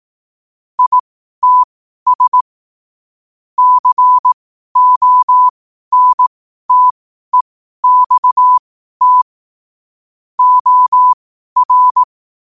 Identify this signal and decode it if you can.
Morse code, heard as a keyed tone, about 18 wpm, spaced as for 12 wpm, 1000 Hz. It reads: ITS CONTEXT OR
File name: qotdfiles / 12wpm